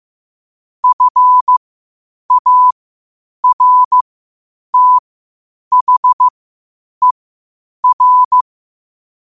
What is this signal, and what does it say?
Morse code, heard as a keyed tone, about 15 wpm, spaced as for 8 wpm, 1000 Hz. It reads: FARTHER
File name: qotdfiles / 8wpm